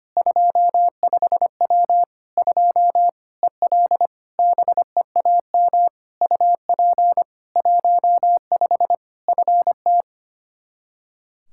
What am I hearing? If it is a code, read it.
Morse code, heard as a keyed tone, about 25 wpm, 700 Hz. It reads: 25W 2 EL BEAM UP 15 FT
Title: K1USN SST QSO Practice 25wpm